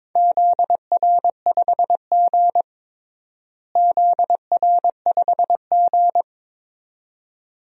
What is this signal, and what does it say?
Morse code, heard as a keyed tone, about 22 wpm, 700 Hz. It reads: ZR5G ZR5G
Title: DXCC-100 Call Signs - 4 Characters - Repeated 2x - 22wpm